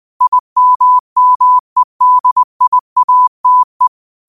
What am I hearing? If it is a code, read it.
Morse code, heard as a keyed tone, about 20 wpm, 1000 Hz. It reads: IMMEDIATE
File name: qotdfiles / 20wpm